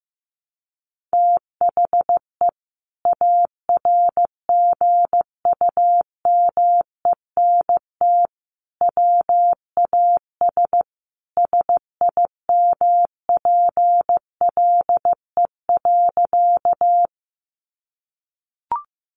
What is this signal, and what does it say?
Morse code, heard as a keyed tone, about 15 wpm, 700 Hz. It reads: THE ARGUMENT WAS SIMPLE.
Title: Sentences from Top 1400 Words 15wpm